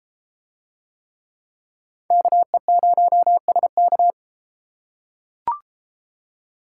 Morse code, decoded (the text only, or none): KE0SK